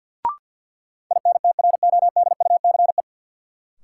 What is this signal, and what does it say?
Morse code, heard as a keyed tone, about 47 wpm, 700 Hz. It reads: INTRODUCE
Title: Sentences from Top 1600 Words - New Word Drill Alt - 50wpm